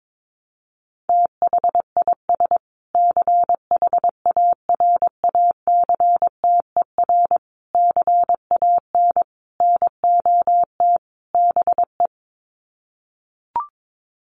Morse code, heard as a keyed tone, about 22 wpm, 700 Hz. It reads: THIS CHARACTER CAN NOT BE
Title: Sets of 5 Words Encore2 22wpm